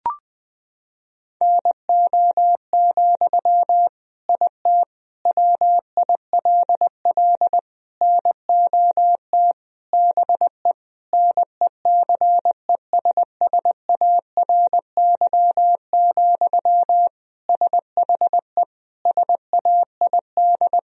Morse code, heard as a keyed tone, about 21 wpm, 700 Hz. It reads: NO, IT WILL NOT BE NECESSARY, SHE SAID
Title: Sentences from Top 1000 Words - Review Part 4 - 20wpm